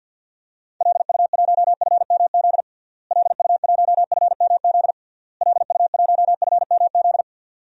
Morse code, heard as a keyed tone, about 52 wpm, 700 Hz. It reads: LU1FMZ LU1FMZ LU1FMZ
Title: DXCC-100 Call Signs - 6 Characters - Repeated 3x - 50wpm